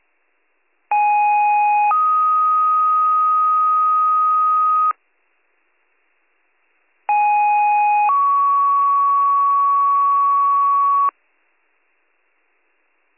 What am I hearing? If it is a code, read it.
Two-tone page, heard as a one-second tone, then a three-second tone, about 830 then 1250 Hz, about 830 then 1120 Hz. It is heard from 0.9 to 11.1 s.